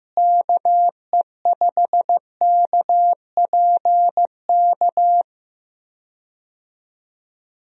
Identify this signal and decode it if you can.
Morse code, heard as a keyed tone, about 15 wpm, 700 Hz. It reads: KE5KPK
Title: USA Call Signs - 6 Characters - Repeated 2x - 15wpm